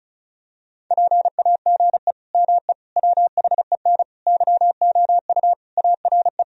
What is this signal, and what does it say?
Morse code, heard as a keyed tone, about 36 wpm, 700 Hz. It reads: PAGE ME WHEN YOU ARE
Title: Sentences from Top 1000 Words - Review Part 1 - 35wpm